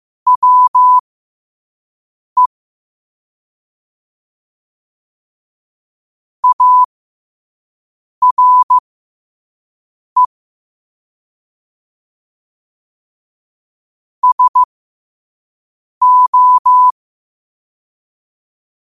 Morse code, heard as a keyed tone, about 15 wpm, spaced as for 5 wpm, 1000 Hz. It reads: WE ARE SO
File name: qotdfiles / 5wpm